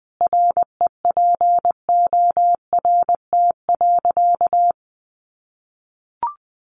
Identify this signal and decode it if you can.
Morse code, heard as a keyed tone, about 20 wpm, 700 Hz. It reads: REPORT.
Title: Sentences from Top 900 Words 20wpm